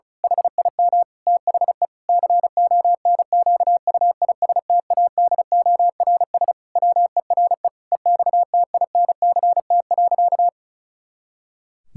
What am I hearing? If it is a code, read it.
Morse code, heard as a keyed tone, about 36 wpm, 700 Hz. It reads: HIM THE CONQUISTADORS WERE EXTINCT.
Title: Pangram Challenge v2 - 35wpm